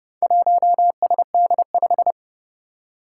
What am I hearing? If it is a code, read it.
Morse code, heard as a keyed tone, about 28 wpm, 700 Hz. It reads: 1SD5
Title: Four Challenging Letter-Number Combinations 30wpm